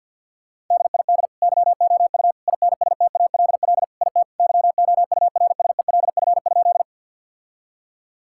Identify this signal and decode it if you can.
Morse code, heard as a keyed tone, about 51 wpm, 700 Hz. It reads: DID YOU INSTALL IT YOURSELF?
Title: Sentences from Top 1600 Words 50wpm